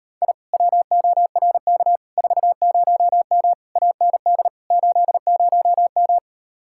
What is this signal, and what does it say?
Morse code, heard as a keyed tone, about 39 wpm, 700 Hz. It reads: I WORK 40M AND 80M